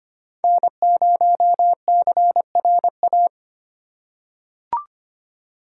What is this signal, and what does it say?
Morse code, heard as a keyed tone, about 25 wpm, 700 Hz. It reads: N0CRA
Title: USA Call Signs - 5 Characters - 4x Character Spacing - 25wpm